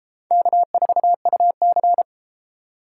Morse code, heard as a keyed tone, about 32 wpm, 700 Hz. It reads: K4UC